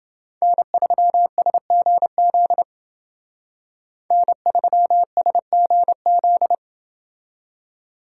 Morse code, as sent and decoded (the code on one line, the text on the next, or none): -. ...-- ... --. --.. / -. ...-- ... --. --..
N3SGZ N3SGZ